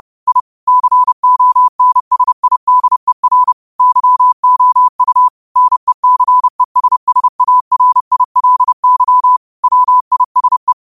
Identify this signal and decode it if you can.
Morse code, heard as a keyed tone, about 30 wpm, 1000 Hz. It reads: I CONSIDER YOU NECESSARILY WISE